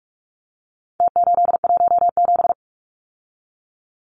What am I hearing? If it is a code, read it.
Morse code, heard as a keyed tone, about 44 wpm, 700 Hz. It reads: T817
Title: Numbers - Contest Style - 4-Digit 45wpm